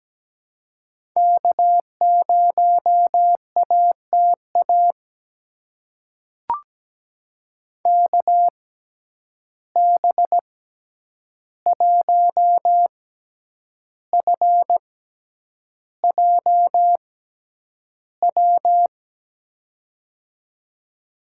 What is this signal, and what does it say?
Morse code, heard as a keyed tone, about 17 wpm, spaced as for 8 wpm, 700 Hz. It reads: K0ATA KB1FJW
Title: USA Call Signs - 6x Character Spacing - 17wpm